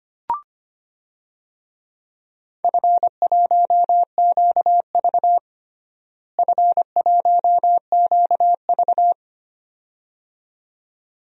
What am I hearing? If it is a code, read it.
Morse code, heard as a keyed tone, about 25 wpm, 700 Hz. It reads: F1QV F1QV
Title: DXCC-100 Call Signs - 4 Characters - Repeated 2x - 25wpm